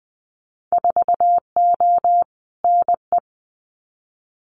Morse code, heard as a keyed tone, about 20 wpm, 700 Hz. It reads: 4O NE
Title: Winter Field Day Exchanges Encore 20wpm